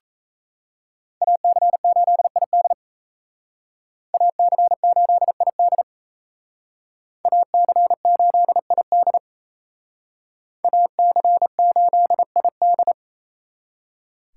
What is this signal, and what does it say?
Morse code, heard as a keyed tone, about 36 wpm, 700 Hz. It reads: AC8ID AC8ID AC8ID AC8ID